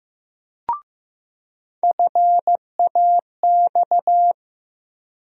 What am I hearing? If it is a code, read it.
Morse code, heard as a keyed tone, about 16 wpm, 700 Hz. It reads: FAX
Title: USA Call Sign Postfixes - 3 Letters 15wpm